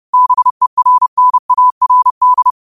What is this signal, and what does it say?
Morse code, heard as a keyed tone, about 30 wpm, 1000 Hz. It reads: BERNARD